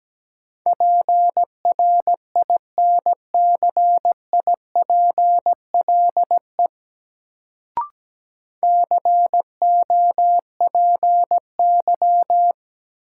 Morse code, heard as a keyed tone, about 17 wpm, 700 Hz. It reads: PRINCIPLE COPY
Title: Sentences from Top 1400 Words - New Word Drill - 17wpm